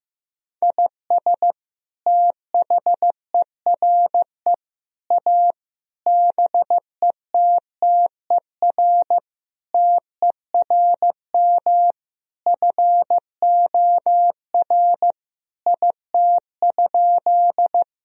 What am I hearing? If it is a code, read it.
Morse code, heard as a keyed tone, about 15 wpm, 700 Hz. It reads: IS THERE A BETTER TERM FOR IT?
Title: Sentences from Top 1000 Words - Review Part 4 - 15wpm